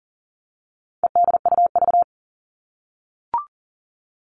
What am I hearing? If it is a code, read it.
Morse code, heard as a keyed tone, about 40 wpm, 700 Hz. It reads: EDUV
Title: All Cut Numbers - 4-Digit 40wpm